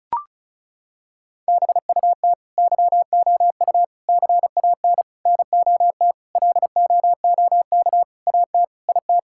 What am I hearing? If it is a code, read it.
Morse code, heard as a keyed tone, about 36 wpm, 690 Hz. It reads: BUT YOU CAN NOT LOOK AT IT
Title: Sentences from Top 1000 Words - Review Part 6 - 35wpm